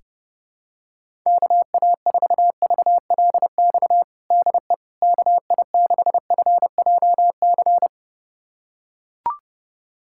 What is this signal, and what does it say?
Morse code, heard as a keyed tone, about 32 wpm, 700 Hz. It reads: KA4VLX DE KI6FJC
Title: USA Callsign DE Callsign - 6 Characters - 30wpm